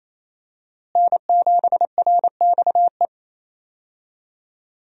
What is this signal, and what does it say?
Morse code, heard as a keyed tone, about 28 wpm, 700 Hz. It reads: N7RXE
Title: USA Call Signs - 5 Characters - Repeated 2x - 28wpm